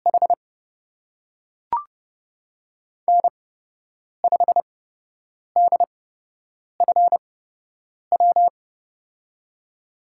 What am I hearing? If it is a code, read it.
Morse code, heard as a keyed tone, about 31 wpm, spaced as for 8 wpm, 700 Hz. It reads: H N5DFW
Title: USA Call Signs - 5 Characters - 8x Character Spacing - 30wpm